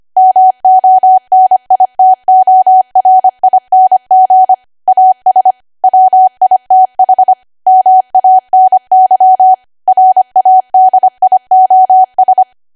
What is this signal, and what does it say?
Morse code, heard as a keyed tone, about 25 wpm, 750 Hz. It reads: MONITORING AS WITH MANY RADIOS
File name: Archive / 25 WPM / 25WPM